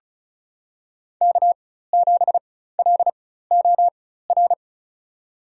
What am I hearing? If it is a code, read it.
Morse code, heard as a keyed tone, about 36 wpm, spaced as for 16 wpm, 700 Hz. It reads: K7LOR
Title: USA Call Signs - 5 Characters - 4x Character Spacing - 35wpm